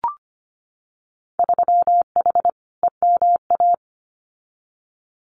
Morse code, heard as a keyed tone, about 25 wpm, 690 Hz. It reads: 3H EMA
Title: Winter Field Day Exchanges Encore 25wpm